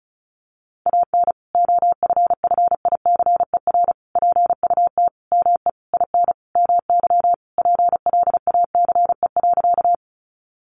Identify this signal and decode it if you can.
Morse code, heard as a keyed tone, about 34 wpm, 700 Hz. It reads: AN OFFICER PUT ME IN MY PLACE.
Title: Sentences from Top 1200 Words 35wpm